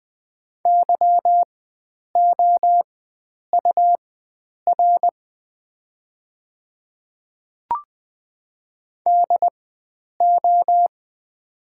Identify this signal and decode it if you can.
Morse code, heard as a keyed tone, about 20 wpm, spaced as for 9 wpm, 690 Hz. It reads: YOUR DO
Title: 4x Character Spacing - Top 100 Words - 20wpm